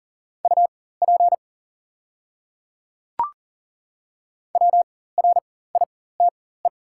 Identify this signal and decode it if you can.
Morse code, heard as a keyed tone, about 40 wpm, spaced as for 19 wpm, 700 Hz. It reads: UP WRITE
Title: 4x Character Spacing - Top 100 Words - 40wpm